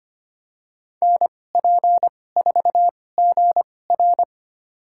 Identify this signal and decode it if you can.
Morse code, heard as a keyed tone, about 25 wpm, spaced as for 18 wpm, 700 Hz. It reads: NP4GR
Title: USA Call Signs - 2x Character Spacing - 25wpm